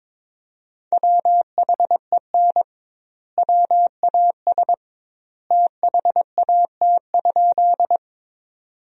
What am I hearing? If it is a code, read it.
Morse code, heard as a keyed tone, about 21 wpm, 700 Hz. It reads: WHEN WAS THAT?
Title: 2x Word-Spacing - Sentences from Top 100 Words - 22wpm